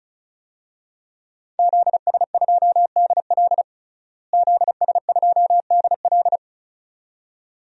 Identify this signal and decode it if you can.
Morse code, heard as a keyed tone, about 36 wpm, 700 Hz. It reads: ZS2DL ZS2DL
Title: DXCC-100 Call Signs - 5 Characters - Repeated 2x - 35wpm